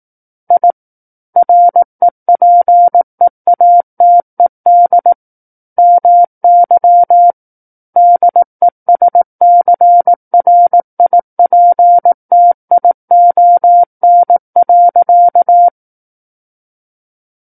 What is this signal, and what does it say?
Morse code, heard as a keyed tone, about 18 wpm, 700 Hz. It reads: I REPEATED MY DESCRIPTION.
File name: 18 WPM